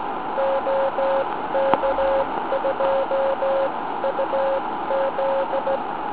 Morse code, heard as a keyed tone, about 16 wpm, spaced as for 14 wpm, 540 Hz, with noise. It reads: OK2UZ